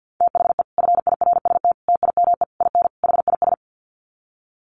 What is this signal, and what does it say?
Morse code, heard as a keyed tone, about 50 wpm, 700 Hz. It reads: THE FIRST TIME IN HIS
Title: Sets of 5 Words Encore2 50wpm